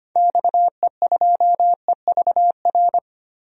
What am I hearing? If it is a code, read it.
Morse code, heard as a keyed tone, about 25 wpm, 690 Hz. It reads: XE2EVR
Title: DXCC-100 Call Signs - 6 Characters - 25wpm